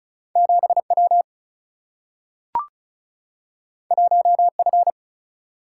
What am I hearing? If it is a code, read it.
Morse code, heard as a keyed tone, about 36 wpm, 700 Hz. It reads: 7W 1F
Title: Single-Letter Single-Number 35wpm